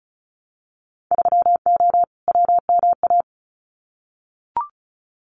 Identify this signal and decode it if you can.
Morse code, heard as a keyed tone, about 36 wpm, 700 Hz. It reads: 3O WMA
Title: Winter Field Day Exchanges Encore 35wpm